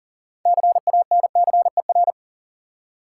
Morse code, heard as a keyed tone, about 41 wpm, 700 Hz. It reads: CANCER
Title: Six Letter Words - 40wpm